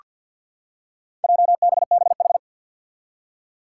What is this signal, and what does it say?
Morse code, heard as a keyed tone, about 47 wpm, 700 Hz. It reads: JBBH
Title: Four Challenging Letter-Number Combinations 50wpm